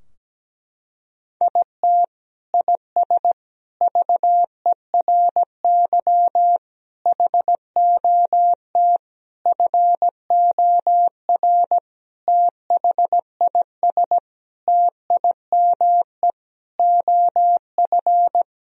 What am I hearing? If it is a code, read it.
Morse code, heard as a keyed tone, about 17 wpm, 700 Hz. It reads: IT IS VERY HOT FOR THIS TIME OF
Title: Sentences from Top 1000 Words - Review Part 3 - 17wpm